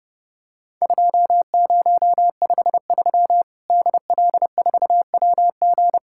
Morse code, heard as a keyed tone, about 30 wpm, 700 Hz. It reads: 2053 DL4WG
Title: Worked All Europe DX Contest - QTC Exchange - 30wpm